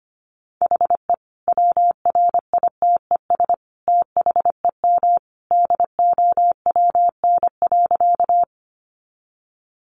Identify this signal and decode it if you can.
Morse code, heard as a keyed tone, about 25 wpm, 700 Hz. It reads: HE WRITES THEM DOWN.
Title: Additional Easy Phrases 25wpm